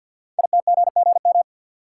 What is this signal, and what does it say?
Morse code, heard as a keyed tone, about 50 wpm, 700 Hz. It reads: ITGGM
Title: Code Groups - 5 Letters - 50wpm